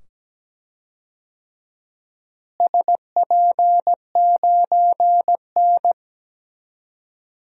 Morse code, heard as a keyed tone, about 17 wpm, 700 Hz. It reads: SP9N
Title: DXCC-100 Call Signs - 4 Characters - Repeated 2x - 17wpm